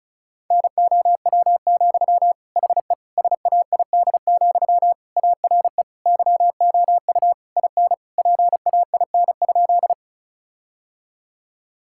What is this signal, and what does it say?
Morse code, heard as a keyed tone, about 36 wpm, 700 Hz. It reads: NOW, HE SAID, ARE YOU IN PAIN?
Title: Sentences from Top 1600 Words 35wpm